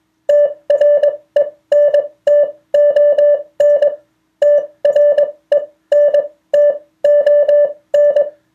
Morse code, heard as a keyed tone, about 21 wpm, spaced as for 17 wpm, 580 Hz. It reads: TRENTON TRENTON